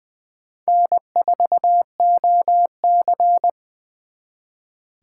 Morse code, heard as a keyed tone, about 26 wpm, 700 Hz. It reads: N4OC